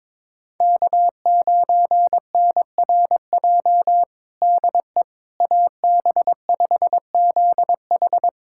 Morse code, heard as a keyed tone, about 22 wpm, 700 Hz. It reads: K9NRJ DE AB5ZH
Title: USA Callsign DE Callsign - 5 Characters - 22wpm